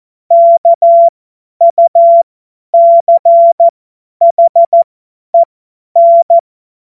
Morse code, heard as a keyed tone, about 14 wpm, spaced as for 10 wpm, 680 Hz. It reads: KUCHEN
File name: Morsekurs / Witze / Witz 09. Februar 2021, 50 F70